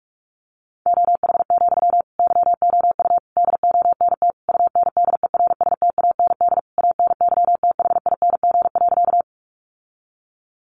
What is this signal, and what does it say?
Morse code, heard as a keyed tone, about 44 wpm, 700 Hz. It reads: OH, YOU DONT UNDERSTAND ANYTHING.
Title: Sentences from Top 1000 Words 45wpm